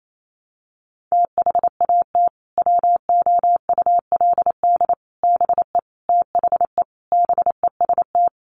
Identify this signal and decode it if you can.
Morse code, heard as a keyed tone, about 28 wpm, 700 Hz. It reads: THAT WOULD BE THE BEST
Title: Sentences from Top 400 Words 28wpm